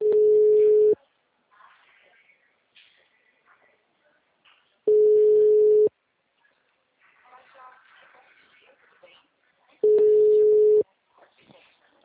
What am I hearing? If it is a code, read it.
Ringback tone (Europe, 425 Hz), with speech.